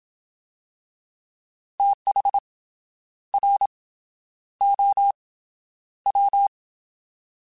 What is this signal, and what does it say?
Morse code, heard as a keyed tone, about 27 wpm, 800 Hz. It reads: TH R O W
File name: TH / TH mp3s / Fast TH @25